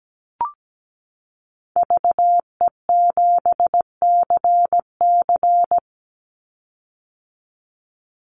Morse code, heard as a keyed tone, about 17 wpm, 700 Hz. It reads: VE7CC
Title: Call Signs - 17wpm